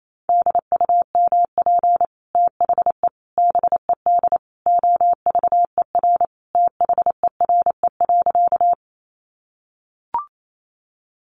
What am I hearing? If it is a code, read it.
Morse code, heard as a keyed tone, about 28 wpm, 700 Hz. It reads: DUMP THE BED OVER THERE.
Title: Sentences from Top 1300 Words 28wpm